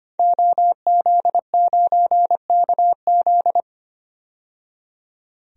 Morse code, heard as a keyed tone, about 25 wpm, 690 Hz. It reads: OZ9KZ
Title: DXCC-100 Call Signs - 5 Characters - 25wpm